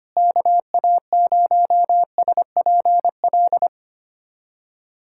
Morse code, heard as a keyed tone, about 25 wpm, 700 Hz. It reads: KA0SPL
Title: USA Callsign DE Callsign - 6 Characters - 25wpm